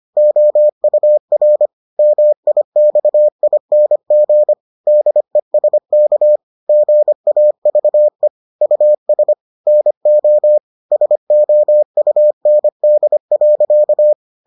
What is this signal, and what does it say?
Morse code, heard as a keyed tone, about 25 wpm, 600 Hz. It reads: OUR MIXING DESK GAVE US NO SOUND.